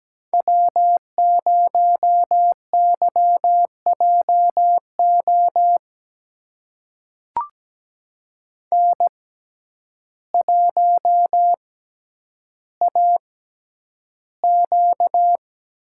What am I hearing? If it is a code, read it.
Morse code, heard as a keyed tone, about 17 wpm, spaced as for 9 wpm, 700 Hz. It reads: W0YJO N1AQ
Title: USA Call Signs - 5 Characters - 6x Character Spacing - 17wpm